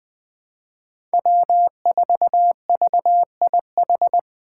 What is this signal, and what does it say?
Morse code, heard as a keyed tone, about 20 wpm, 700 Hz. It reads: W4VIH
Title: Code Groups - 5 Letter:Numbers - 20wpm